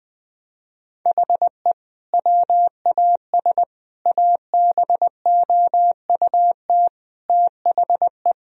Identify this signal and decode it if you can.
Morse code, heard as a keyed tone, about 20 wpm, 700 Hz. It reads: HE WAS ABOUT THE
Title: Sentences from Top 1000 Words - Review Part 3 - 20wpm